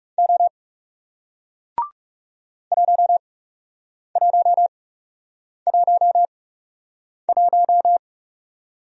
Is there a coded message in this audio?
Yes: Morse code from 0.2 to 8.0 s, heard as a keyed tone, about 39 wpm, spaced as for 8 wpm, 700 Hz. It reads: O 1111